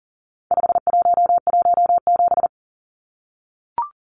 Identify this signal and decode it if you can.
Morse code, heard as a keyed tone, about 40 wpm, 700 Hz. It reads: 5117
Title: Numbers - 4-Digit 40wpm